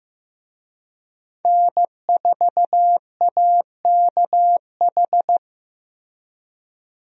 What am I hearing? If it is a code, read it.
Morse code, heard as a keyed tone, about 15 wpm, 700 Hz. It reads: N4AKH
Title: USA Call Signs - 5 Characters - Repeated 3x - 15wpm